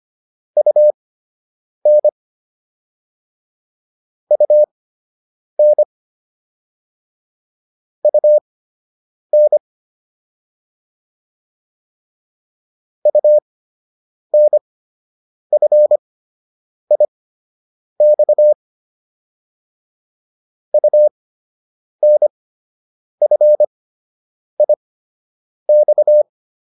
Morse code, heard as a keyed tone, about 25 wpm, spaced as for 8 wpm, 600 Hz. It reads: UN UN UN UNFIX UNFIX